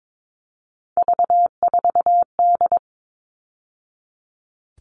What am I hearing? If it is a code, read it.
Morse code, heard as a keyed tone, about 22 wpm, 700 Hz. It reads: V4D
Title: All Cut Numbers - 3-Digit 22wpm